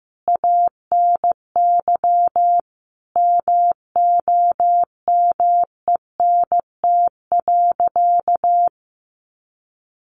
Morse code, heard as a keyed tone, about 15 wpm, 700 Hz. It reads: ANY MOMENT.
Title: Sentences from Top 1100 Words 15wpm